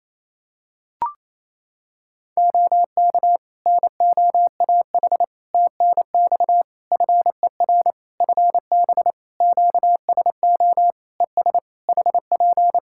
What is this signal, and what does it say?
Morse code, heard as a keyed tone, about 27 wpm, 700 Hz. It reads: OK NOAH TNX FER FB QSO ES HP
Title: Practice QSO - Exchange 4 - Ending - 28wpm